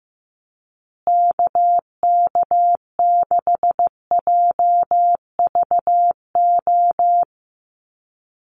Morse code, heard as a keyed tone, about 15 wpm, 700 Hz. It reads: KK6JVO
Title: USA Call Signs Repeated 3x 15wpm